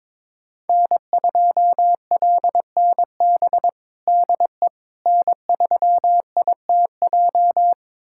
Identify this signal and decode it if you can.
Morse code, heard as a keyed tone, about 22 wpm, 700 Hz. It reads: N2LNB DE N3ITJ
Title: USA Callsign DE Callsign - 5 Characters - 22wpm